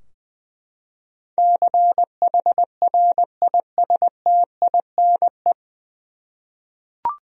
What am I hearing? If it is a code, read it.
Morse code, heard as a keyed tone, about 21 wpm, 700 Hz. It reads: CHRISTINE
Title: USA Common Names - 20wpm